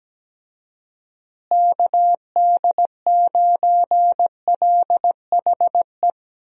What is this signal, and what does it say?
Morse code, heard as a keyed tone, about 17 wpm, 700 Hz. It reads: KD9LHE
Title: USA Call Signs - 6 Characters - Repeated 2x - 17wpm